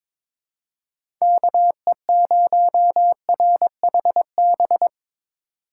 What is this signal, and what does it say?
Morse code, heard as a keyed tone, about 22 wpm, 700 Hz. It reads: KE0RHB
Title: USA Call Signs - 6 Characters - 2x Character Spacing - 22wpm